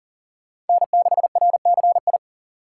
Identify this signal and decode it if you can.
Morse code, heard as a keyed tone, about 52 wpm, 700 Hz. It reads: N6RCI